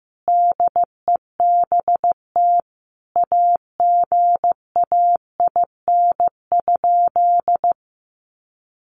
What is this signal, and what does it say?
Morse code, heard as a keyed tone, about 15 wpm, 700 Hz. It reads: DEBT AGAIN?
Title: Sentences from Top 1300 Words 15wpm